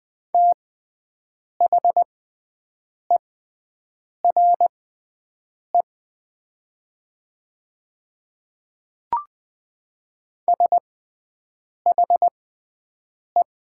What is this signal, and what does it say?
Morse code, heard as a keyed tone, about 20 wpm, spaced as for 7 wpm, 700 Hz. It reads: THERE SHE
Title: 6x Character Spacing - Top 100 Words - 20wpm